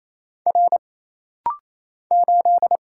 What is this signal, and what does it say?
Morse code, heard as a keyed tone, about 28 wpm, 700 Hz. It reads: R 8
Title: Single Letter-Number Alt - Rapid-Fire 28wpm